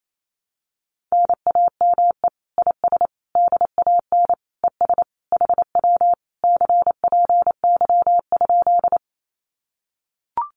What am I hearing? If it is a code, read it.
Morse code, heard as a keyed tone, about 28 wpm, 700 Hz. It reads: NAME IS DAN ES HW CPY?
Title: Common QSO Phrases Encore 28wpm